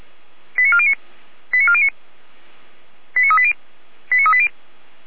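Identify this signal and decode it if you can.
Five-tone selective call (ZVEI), heard as quick runs of five tones, tones of about 70 ms. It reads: 89409 89409 89380 89380
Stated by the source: ZVEI five-tone selective call received on SDR.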